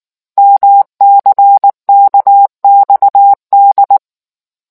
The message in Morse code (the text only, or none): MCKXD